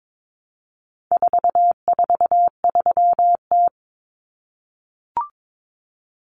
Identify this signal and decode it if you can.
Morse code, heard as a keyed tone, about 22 wpm, 700 Hz. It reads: 443T
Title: Common Cut Numbers - 4-Digit 22wpm